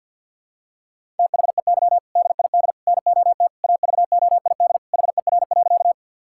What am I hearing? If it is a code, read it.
Morse code, heard as a keyed tone, about 52 wpm, 700 Hz. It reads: THEY DID NOT AVOID HER.
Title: Sentences from Top 1000 Words - Review Part 6 - 50wpm